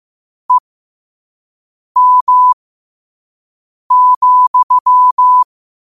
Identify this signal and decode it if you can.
Morse code, heard as a keyed tone, about 15 wpm, spaced as for 5 wpm, 1000 Hz. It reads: EM,
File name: qotdfiles / 5wpm